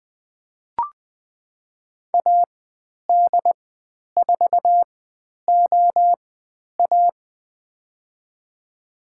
Morse code, heard as a keyed tone, about 20 wpm, spaced as for 10 wpm, 700 Hz. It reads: AD4OA
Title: 095 Lesson 16g - Letter c - Call Signs 20wpm